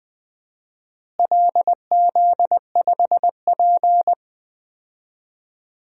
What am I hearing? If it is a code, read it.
Morse code, heard as a keyed tone, about 20 wpm, 700 Hz. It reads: LZ5P
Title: DXCC-100 Call Signs - 4 Characters - Repeated 2x - 20wpm